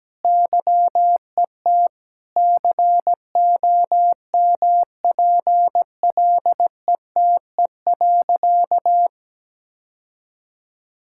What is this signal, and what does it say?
Morse code, heard as a keyed tone, about 17 wpm, 700 Hz. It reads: YET COMPLETE.
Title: Sentences from Top 1000 Words - Review Part 6 - 17wpm